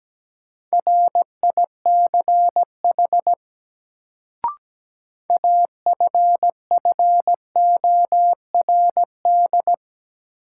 Morse code, heard as a keyed tone, about 17 wpm, 700 Hz. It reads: RICH AFFORD
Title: Sentences from Top 1100 Words - New Word Drill - 17wpm